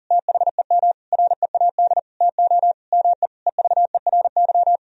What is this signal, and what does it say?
Morse code, heard as a keyed tone, about 40 wpm, 700 Hz. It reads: THEM READ TO ME EVERY
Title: Sentences from Top 1000 Words - Review Part 1 - 40wpm